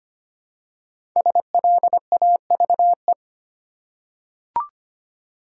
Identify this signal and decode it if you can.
Morse code, heard as a keyed tone, about 25 wpm, 700 Hz. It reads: SLAVE
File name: 5 Letter Words - First 3 Letters Given - 25wpm